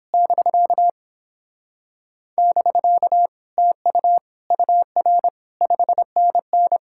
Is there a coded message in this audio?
Yes: Morse code from 0.1 to 6.8 s, heard as a keyed tone, about 29 wpm, 700 Hz. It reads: <BK> <BK> TU UR 5NN